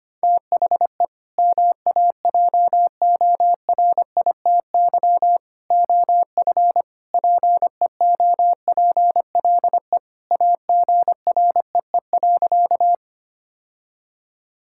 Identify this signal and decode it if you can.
Morse code, heard as a keyed tone, about 25 wpm, 700 Hz. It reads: THE MAJORITY OF PEOPLE AGREE.
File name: practice / 25wpm / Sentences with 3-Word Phrases - 25wpm